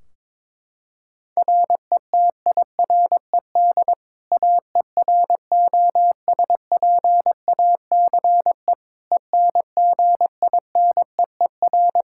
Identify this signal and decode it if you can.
Morse code, heard as a keyed tone, about 21 wpm, 700 Hz. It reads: RETIRED AEROSPACE ENGINEER
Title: K1USN SST QSO Practice 22wpm